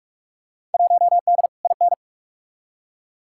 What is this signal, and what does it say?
Morse code, heard as a keyed tone, about 46 wpm, 700 Hz. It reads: 1D IN
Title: Field Day Exchanges Encore - 45wpm